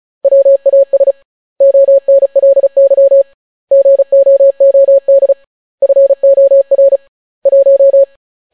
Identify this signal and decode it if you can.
Morse code, heard as a keyed tone, about 36 wpm, 550 Hz. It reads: WAS ONLY GOOD FOR 1